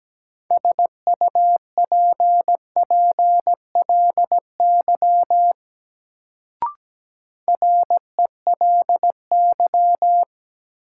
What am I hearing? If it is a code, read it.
Morse code, heard as a keyed tone, about 17 wpm, 700 Hz. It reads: SUPPLY RELY
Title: Sentences from Top 1200 Words - New Word Drill Alt - 17wpm